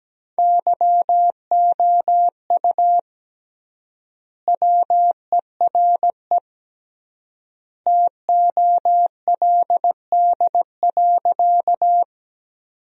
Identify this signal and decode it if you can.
Morse code, heard as a keyed tone, about 17 wpm, 700 Hz. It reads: YOU WERE TOLD.
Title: 3x Word-Spacing - Sentences from Top 400 Words - 17wpm